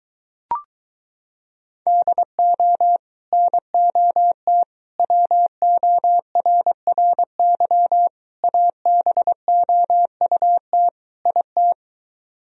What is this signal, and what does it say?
Morse code, heard as a keyed tone, about 24 wpm, 700 Hz. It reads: DO NOT WORRY ABOUT IT